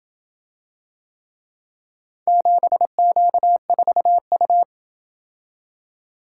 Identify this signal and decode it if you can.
Morse code, heard as a keyed tone, about 27 wpm, 700 Hz. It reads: 7Q4U